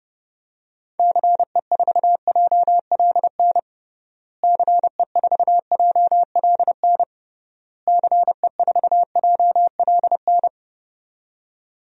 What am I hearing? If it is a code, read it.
Morse code, heard as a keyed tone, about 30 wpm, 700 Hz. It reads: CE4JLN CE4JLN CE4JLN
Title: DXCC-100 Call Signs - 6 Characters - Repeated 3x - 30wpm